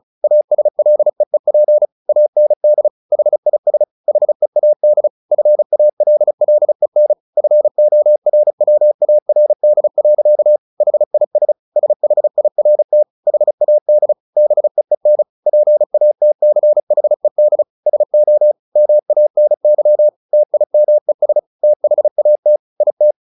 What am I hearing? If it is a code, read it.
Morse code, heard as a keyed tone, about 34 wpm, 600 Hz. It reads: ASLEEP AND HIS HEAD FALLEN FORWARD. HIS SHIRT HAD BEEN PATCHED SO MANY TIMES THAT IT
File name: the old man and the sea / the old man and the sea 1 35wpm